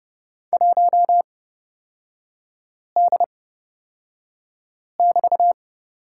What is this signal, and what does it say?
Morse code, heard as a keyed tone, about 30 wpm, spaced as for 5 wpm, 700 Hz. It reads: 1D<BT>
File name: Lesson 36e - BT - Three Characters 30wpm